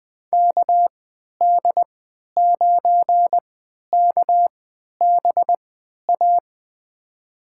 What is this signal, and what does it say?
Morse code, heard as a keyed tone, about 20 wpm, spaced as for 11 wpm, 700 Hz. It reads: KD9KBA
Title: USA Call Signs - 6 Characters - 3x Character Spacing - 20wpm